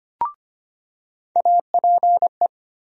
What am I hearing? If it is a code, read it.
Morse code, heard as a keyed tone, about 25 wpm, 700 Hz. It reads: APE
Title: USA Call Sign Postfixes - 3 Letters 25wpm